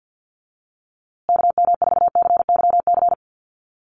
Morse code, heard as a keyed tone, about 50 wpm, 700 Hz. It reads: KM4CYC
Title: USA Call Signs Repeated 3x 50wpm